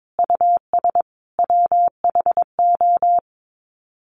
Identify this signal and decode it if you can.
Morse code, heard as a keyed tone, about 22 wpm, 700 Hz. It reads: US WHO
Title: Sets of 5 Words Encore 22wpm